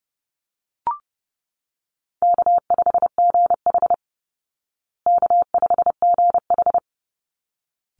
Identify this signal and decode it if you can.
Morse code, heard as a keyed tone, about 30 wpm, 700 Hz. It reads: K5GH K5GH
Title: Easy Call Signs Repeated 2x - 30wpm